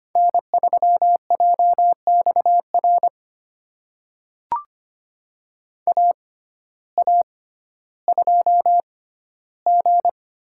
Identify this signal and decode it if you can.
Morse code, heard as a keyed tone, about 25 wpm, spaced as for 14 wpm, 700 Hz. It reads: N3JXR AA2G
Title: USA Call Signs - 5 Characters - 6x Character Spacing - 25wpm